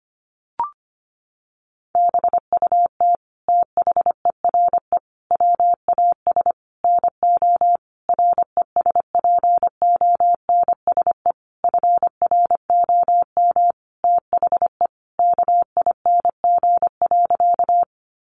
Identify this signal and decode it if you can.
Morse code, heard as a keyed tone, about 25 wpm, 700 Hz. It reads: BUT THERE WAS NO RESPONSE FROM THE KING.
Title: Sentences from Top 1400 Words 25wpm